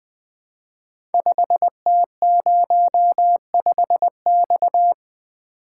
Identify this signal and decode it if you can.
Morse code, heard as a keyed tone, about 20 wpm, 700 Hz. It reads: 5T05X
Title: Code Groups - 5 Letter:Numbers - 20wpm